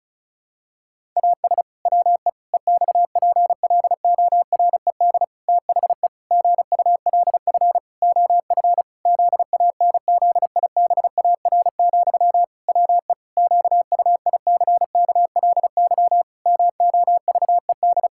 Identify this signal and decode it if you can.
Morse code, heard as a keyed tone, about 34 wpm, 700 Hz. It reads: AS WE EXPLORED THE GULF OF ZANZIBAR, WE QUICKLY MOVED
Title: Pangram Challenge v2 - 35wpm